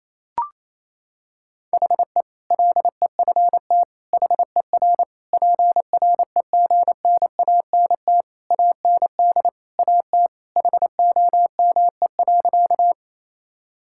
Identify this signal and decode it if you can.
Morse code, heard as a keyed tone, about 27 wpm, 700 Hz. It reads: HE LEFT HER PREGNANT AND AT HOME.
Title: Sentences from Top 1000 Words - Review Part 6 - 28wpm